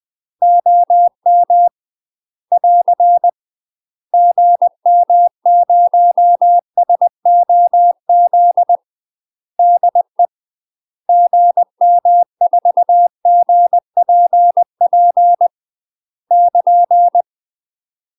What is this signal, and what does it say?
Morse code, heard as a keyed tone, about 20 wpm, 700 Hz. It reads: OM <AR> GM0SOZ DE GM4GPP <KN>
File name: UK MASTER QSOs TRAINING / 20WPM / UK MASTER QSOs 0071